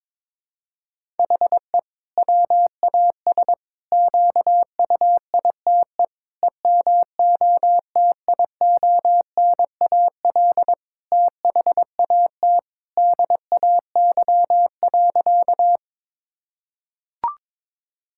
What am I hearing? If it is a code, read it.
Morse code, heard as a keyed tone, about 22 wpm, 700 Hz. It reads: HE WAS QUITE EMOTIONAL THAT DAY.
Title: Sentences from Top 1000 Words - Review Part 5 - 22wpm